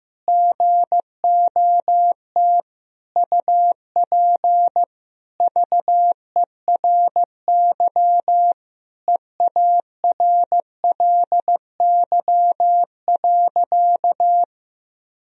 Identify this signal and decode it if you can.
Morse code, heard as a keyed tone, about 16 wpm, 700 Hz. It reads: GOT UP VERY EARLY.
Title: Sentences from Top 1000 Words - Review Part 2 - 15wpm